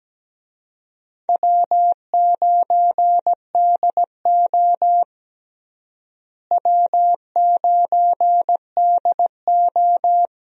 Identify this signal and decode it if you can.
Morse code, heard as a keyed tone, about 17 wpm, 700 Hz. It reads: W9DO W9DO
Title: USA Call Signs - 4 Characters - Repeated 3x - 17wpm